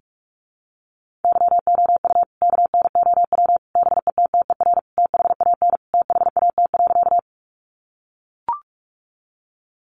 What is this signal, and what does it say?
Morse code, heard as a keyed tone, about 44 wpm, 700 Hz. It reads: YOU KNOW BETTER THAN THAT.
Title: Sentences from Top 400 Words 45wpm